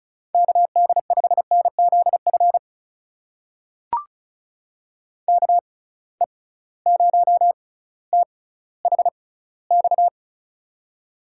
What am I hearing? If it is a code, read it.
Morse code, heard as a keyed tone, about 36 wpm, spaced as for 18 wpm, 700 Hz. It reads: KD5NZF KE0THX
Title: USA Call Signs - 6 Characters - 6x Character Spacing - 35wpm